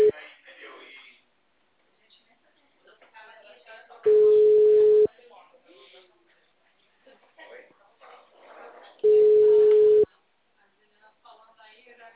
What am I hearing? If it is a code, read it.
Ringback tone (Europe, 425 Hz), with speech.